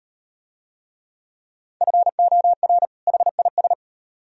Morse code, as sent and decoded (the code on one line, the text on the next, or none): ..-. --- .-. / .... .. ...
FOR HIS